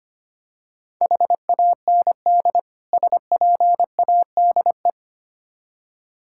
Morse code, heard as a keyed tone, about 27 wpm, 700 Hz. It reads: HAND SPADE
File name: practice / 25wpm / Difficult Words - Speed-Racing with Hints - 25wpm